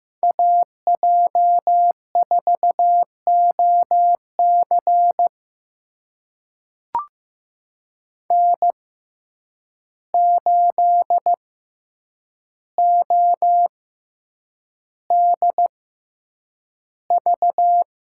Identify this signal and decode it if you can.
Morse code, heard as a keyed tone, about 15 wpm, spaced as for 8 wpm, 700 Hz. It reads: AJ4OC N8ODV
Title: USA Call Signs - 5 Characters - 6x Character Spacing - 15wpm